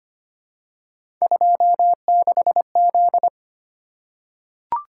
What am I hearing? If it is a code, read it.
Morse code, heard as a keyed tone, about 25 wpm, 700 Hz. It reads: 26Z
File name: Three Letter-Number Combinations - 25wpm